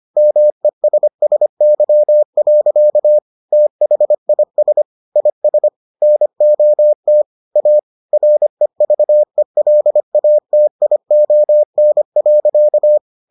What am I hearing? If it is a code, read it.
Morse code, heard as a keyed tone, about 25 wpm, 600 Hz. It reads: MESSY. THIS IS NOT A REVELATION.